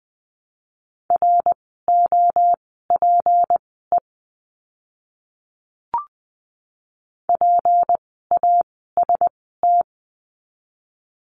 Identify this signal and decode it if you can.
Morse code, heard as a keyed tone, about 20 wpm, spaced as for 14 wpm, 700 Hz. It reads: ROPE PAST
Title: 153 Lesson 25f - Letter p - Words 20wpm